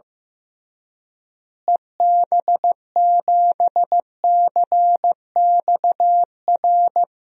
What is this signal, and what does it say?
Morse code, heard as a keyed tone, about 15 wpm, 700 Hz. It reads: EB7CXR
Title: DXCC-100 Call Signs - 6 Characters - Repeated 3x - 15wpm